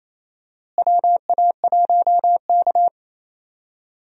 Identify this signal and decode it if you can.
Morse code, heard as a keyed tone, about 28 wpm, 700 Hz. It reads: WA1K
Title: USA Call Signs - 4 Characters - Repeated 3x - 28wpm